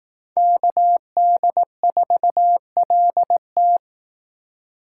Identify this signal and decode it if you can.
Morse code, heard as a keyed tone, about 18 wpm, 700 Hz. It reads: KD4LT
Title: Speed Racing - USA 2x2 Call Signs 15wpm